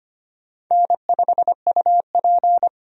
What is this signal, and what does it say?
Morse code, heard as a keyed tone, about 25 wpm, 700 Hz. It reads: N5UP
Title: USA Call Signs - 4 Characters - 25wpm